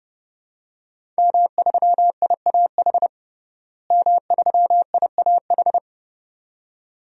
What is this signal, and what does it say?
Morse code, heard as a keyed tone, about 32 wpm, 700 Hz. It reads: M3IAH M3IAH
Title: DXCC-100 Call Signs - 5 Characters - Repeated 2x - 30wpm